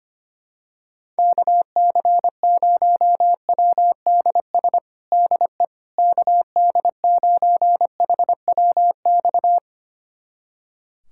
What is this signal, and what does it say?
Morse code, heard as a keyed tone, about 25 wpm, 700 Hz. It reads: KC0WDS DE KD9HWX
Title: USA Callsign DE Callsign - 6 Characters - 25wpm